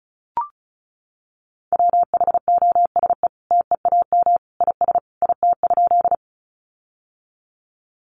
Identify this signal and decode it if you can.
Morse code, heard as a keyed tone, about 34 wpm, 700 Hz. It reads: WHOSE TEAM IS IT?
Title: Sentences from Top 1500 Words 35wpm